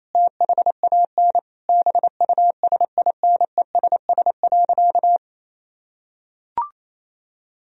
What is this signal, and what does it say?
Morse code, heard as a keyed tone, about 28 wpm, 700 Hz. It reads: THAN BUSINESS.
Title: Sentences from Top 1000 Words - Review Part 2 - 28wpm